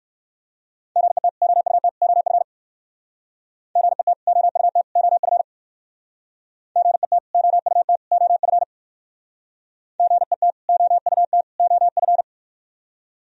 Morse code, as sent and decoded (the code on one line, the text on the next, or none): --. - / --- ..- - / --- ..-. / --. - / --- ..- - / --- ..-. / --. - / --- ..- - / --- ..-. / --. - / --- ..- - / --- ..-.
GET OUT OF GET OUT OF GET OUT OF GET OUT OF